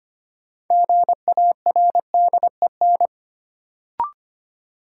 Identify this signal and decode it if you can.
Morse code, heard as a keyed tone, about 25 wpm, 700 Hz. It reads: GARDEN
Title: Sentences from Top 700 Words - New Word Drill - 25wpm